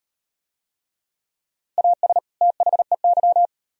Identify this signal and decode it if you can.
Morse code, heard as a keyed tone, about 35 wpm, 700 Hz. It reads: AS THEY